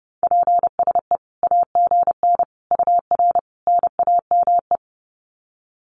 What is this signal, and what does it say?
Morse code, heard as a keyed tone, about 30 wpm, 700 Hz. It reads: PSE AGN UR NAME
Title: Common QSO Phrases 30wpm